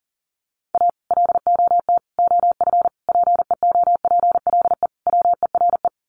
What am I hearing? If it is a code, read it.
Morse code, heard as a keyed tone, about 40 wpm, 700 Hz. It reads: A LOT OF PEOPLE WERE
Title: Sets of 5 Words Encore2 40wpm